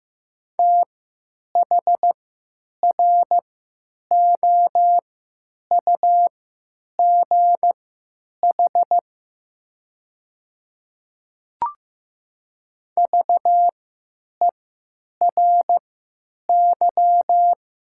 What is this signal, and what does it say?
Morse code, heard as a keyed tone, about 15 wpm, spaced as for 9 wpm, 700 Hz. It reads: THROUGH VERY
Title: 3x Character Spacing - Top 200 Words - 15wpm